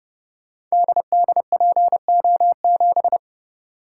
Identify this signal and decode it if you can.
Morse code, heard as a keyed tone, about 30 wpm, 700 Hz. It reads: DDPO7
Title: Code Groups - 5 Letter:Numbers - 30wpm